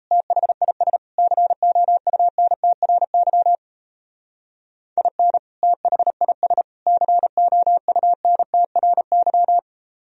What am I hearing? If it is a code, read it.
Morse code, heard as a keyed tone, about 37 wpm, 700 Hz. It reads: THIS COUNTRY IN THIS COUNTRY